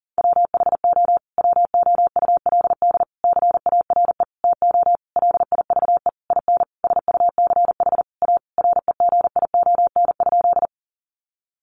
Morse code, heard as a keyed tone, about 40 wpm, 700 Hz. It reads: WHO WOULD CARE TO LIVE IN SUCH A REGION?
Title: Sentences from Top 1500 Words 40wpm